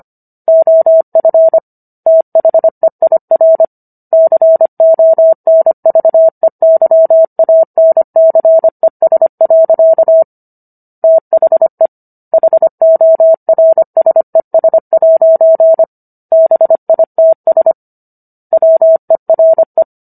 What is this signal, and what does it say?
Morse code, heard as a keyed tone, about 25 wpm, 650 Hz. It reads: OF THEIR CONVEYANCES. THE HORSES' BITS WERE